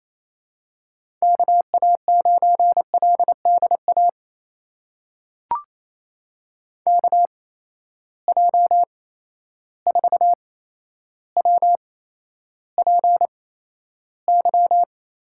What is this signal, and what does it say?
Morse code, heard as a keyed tone, about 28 wpm, spaced as for 12 wpm, 700 Hz. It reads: KA9LDA KJ4WPY
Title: USA Call Signs - 6 Characters - 8x Character Spacing - 28wpm